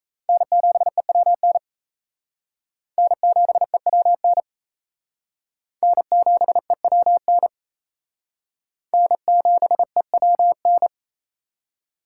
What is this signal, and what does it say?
Morse code, heard as a keyed tone, about 34 wpm, 700 Hz. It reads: N7EWN N7EWN N7EWN N7EWN
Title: Speed Racing - USA 1x3 Call Signs 28wpm